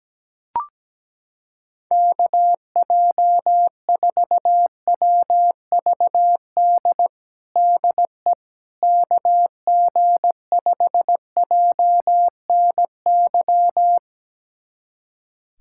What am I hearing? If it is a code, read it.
Morse code, heard as a keyed tone, about 17 wpm, 700 Hz. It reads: KJ4WVD DE KG5JNY
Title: USA Callsign DE Callsign - 6 Characters - 17wpm